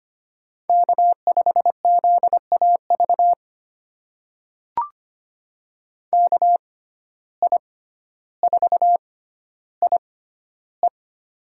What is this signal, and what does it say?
Morse code, heard as a keyed tone, about 25 wpm, spaced as for 13 wpm, 690 Hz. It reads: K5ZAV KI4IE
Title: USA Call Signs - 5 Characters - 6x Character Spacing - 25wpm